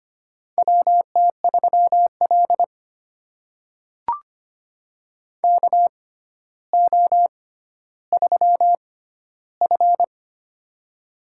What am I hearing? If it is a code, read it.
Morse code, heard as a keyed tone, about 25 wpm, spaced as for 13 wpm, 700 Hz. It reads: WT3L KO3F
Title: USA Call Signs - 4 Characters - 6x Character Spacing - 25wpm